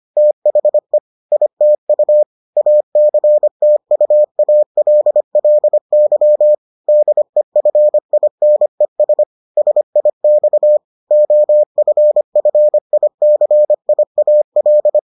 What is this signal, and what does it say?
Morse code, heard as a keyed tone, about 25 wpm, 600 Hz. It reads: THE ITU ACTUALLY DEFINES SIX OFFICIAL